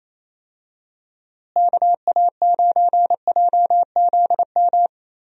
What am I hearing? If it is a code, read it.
Morse code, heard as a keyed tone, about 28 wpm, 700 Hz. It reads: KA9JZM
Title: USA Call Signs - 6 Characters - Repeated 2x - 28wpm